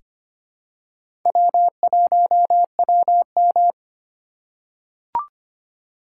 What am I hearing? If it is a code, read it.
Morse code, heard as a keyed tone, about 25 wpm, 700 Hz. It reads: W1WM
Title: USA Call Signs - 4 Characters - 3x Character Spacing - 25wpm